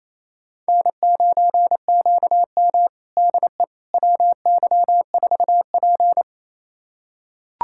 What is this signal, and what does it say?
Morse code, heard as a keyed tone, about 29 wpm, 690 Hz. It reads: N9QM DE WY4P
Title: USA Callsign DE Callsign - 4 Characters - 28wpm